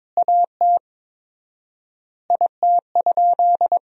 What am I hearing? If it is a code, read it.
Morse code, heard as a keyed tone, about 21 wpm, 700 Hz. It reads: AT IT?
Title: 4x Word-Spacing - Sentences from Top 100 Words - 22wpm